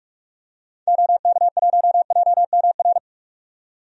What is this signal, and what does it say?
Morse code, heard as a keyed tone, about 42 wpm, 700 Hz. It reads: OK1JMR
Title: DXCC-100 Call Signs - 6 Characters - 45wpm